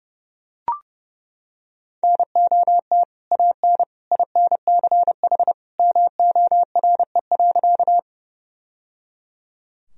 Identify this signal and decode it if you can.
Morse code, heard as a keyed tone, about 30 wpm, 700 Hz. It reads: NOT AN INCH MORE.
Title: Sentences from Top 1000 Words - Review Part 3 - 30wpm